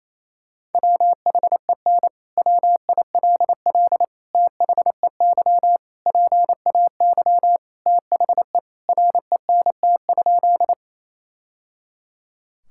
Morse code, heard as a keyed tone, about 28 wpm, 700 Hz. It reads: WHEN WILL THEY PAY THE RENT?
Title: Sentences from Top 1000 Words - Review Part 5 - 28wpm